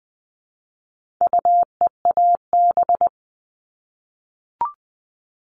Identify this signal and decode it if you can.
Morse code, heard as a keyed tone, about 20 wpm, 700 Hz. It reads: UEAB
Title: All Cut Numbers - 4-Digit 20wpm